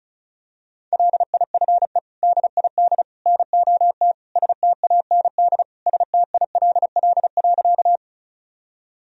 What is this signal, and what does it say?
Morse code, heard as a keyed tone, about 36 wpm, 700 Hz. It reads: LIFE DID NOT STAND STILL.
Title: Sentences from Top 1000 Words - Review Part 1 - 35wpm